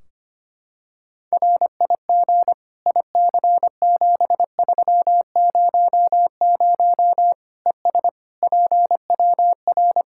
Morse code, heard as a keyed tone, about 25 wpm, 700 Hz. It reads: RIG IC7300 ES PWR
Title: Practice QSO - Exchange 2 - Info - 25wpm